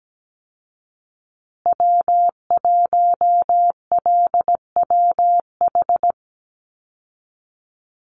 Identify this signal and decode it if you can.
Morse code, heard as a keyed tone, about 17 wpm, 700 Hz. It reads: W1LWH
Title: Call Signs Repeated 3x - 17wpm